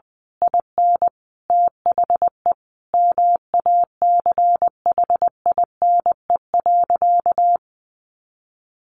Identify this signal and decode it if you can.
Morse code, heard as a keyed tone, about 20 wpm, 700 Hz. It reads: IN THE MACHINE.
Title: A Sound Of Thunder 20wpm